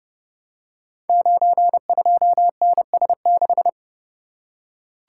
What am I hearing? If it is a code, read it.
Morse code, heard as a keyed tone, about 30 wpm, 700 Hz. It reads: 92NS6
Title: Code Groups - 5 Letter:Numbers - 30wpm